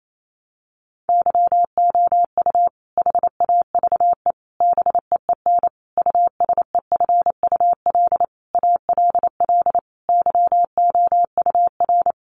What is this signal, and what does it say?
Morse code, heard as a keyed tone, about 28 wpm, 700 Hz. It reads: YOU HAVE BEEN USEFUL ALL YOUR
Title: Sentences from Top 700 Words 28wpm